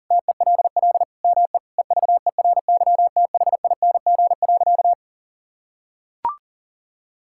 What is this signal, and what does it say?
Morse code, heard as a keyed tone, about 37 wpm, 700 Hz. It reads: TELL ME EVERYTHING.
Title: Sentences from Top 1600 Words 40wpm